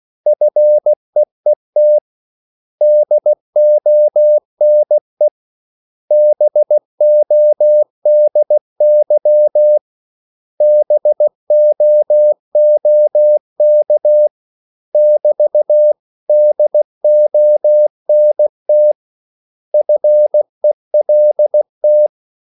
Morse code, heard as a keyed tone, about 16 wpm, 600 Hz. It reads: FEET DONE BODY BOOK <BT> DONT FELT
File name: CW / Practice / 500MostCommonEnglishWords16